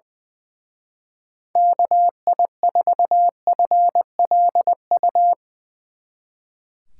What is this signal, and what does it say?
Morse code, heard as a keyed tone, about 22 wpm, 700 Hz. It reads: KI4FLU
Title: Speed Racing - USA Call Signs 20wpm